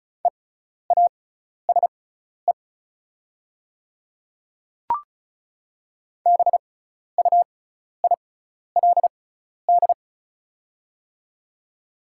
Morse code, heard as a keyed tone, about 36 wpm, spaced as for 12 wpm, 700 Hz. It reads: EASE BUILD
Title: 6x Character Spacing - Top 300 Words - 35wpm